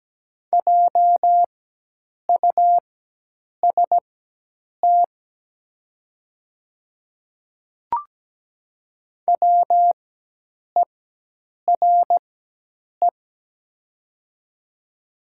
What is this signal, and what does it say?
Morse code, heard as a keyed tone, about 17 wpm, spaced as for 8 wpm, 700 Hz. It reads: JUST WERE
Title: 4x Character Spacing - Top 200 Words - 17wpm